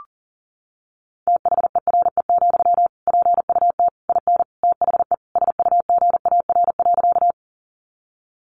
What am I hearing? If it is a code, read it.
Morse code, heard as a keyed tone, about 40 wpm, 700 Hz. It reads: THERE, PUT IN THE SUGAR.
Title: Sentences from Top 1200 Words 40wpm